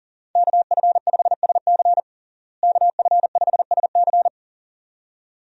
Morse code, heard as a keyed tone, about 40 wpm, 700 Hz. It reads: KF5SC KF5SC
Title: USA Call Signs - 5 Characters - Repeated 2x - 40wpm